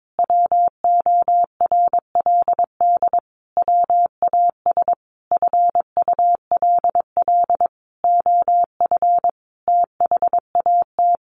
Morse code, heard as a keyed tone, about 22 wpm, 700 Hz. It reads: WORLD WAS FULL OF THAT
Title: Sentences from Top 1300 Words 22wpm